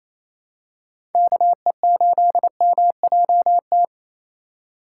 Morse code, heard as a keyed tone, about 28 wpm, 700 Hz. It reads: KE8MJT
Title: USA Call Signs - 6 Characters - 2x Character Spacing - 28wpm